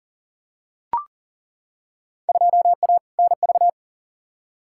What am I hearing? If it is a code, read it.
Morse code, heard as a keyed tone, about 40 wpm, 700 Hz. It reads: 2A NV
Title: Field Day Exchanges - 40wpm